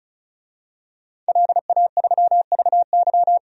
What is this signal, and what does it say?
Morse code, heard as a keyed tone, about 36 wpm, 700 Hz. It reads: LA3VY
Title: DXCC-100 Call Signs - 5 Characters - 35wpm